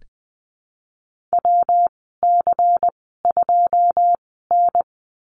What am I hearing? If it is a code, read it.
Morse code, heard as a keyed tone, about 20 wpm, spaced as for 15 wpm, 700 Hz. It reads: WC2N
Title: 147 Lesson 24g - Letter y - Call Signs 20wpm